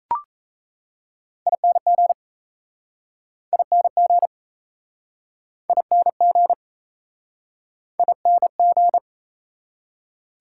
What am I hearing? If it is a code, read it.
Morse code, heard as a keyed tone, about 37 wpm, 700 Hz. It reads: ING ING ING ING